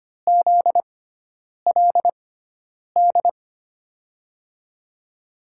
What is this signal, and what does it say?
Morse code, heard as a keyed tone, about 25 wpm, spaced as for 9 wpm, 700 Hz. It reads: ZLD
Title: USA Call Signs - 6 Characters - 6x Character Spacing - 25wpm